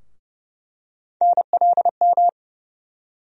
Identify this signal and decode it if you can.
Morse code, heard as a keyed tone, about 31 wpm, 700 Hz. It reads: NLM
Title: USA Call Sign Postfixes - 3 Letters 30wpm